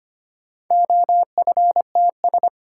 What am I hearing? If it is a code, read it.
Morse code, heard as a keyed tone, about 25 wpm, 700 Hz. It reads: OFTS
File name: practice / 25wpm / Four Letter-Number Combinations - 25wpm